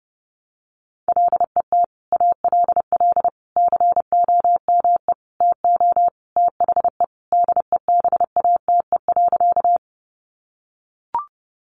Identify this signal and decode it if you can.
Morse code, heard as a keyed tone, about 30 wpm, 700 Hz. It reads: LET ALL COME TO THE DEBATE.
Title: Sentences from Top 1100 Words 30wpm